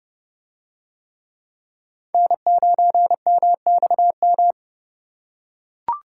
Morse code, heard as a keyed tone, about 29 wpm, 700 Hz. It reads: N9MXM